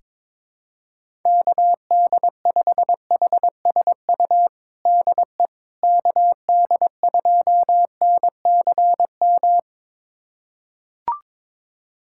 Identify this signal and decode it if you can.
Morse code, heard as a keyed tone, about 22 wpm, 700 Hz. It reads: KD5HSU DE KD2NCM
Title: USA Callsign DE Callsign - 6 Characters - 22wpm